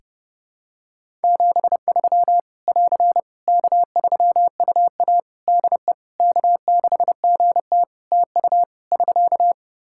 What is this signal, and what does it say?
Morse code, heard as a keyed tone, about 32 wpm, 700 Hz. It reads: 73 <AR> K3UA DE K6GT TU <SK>
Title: Practice QSO - Exchange 4 - Ending - 30wpm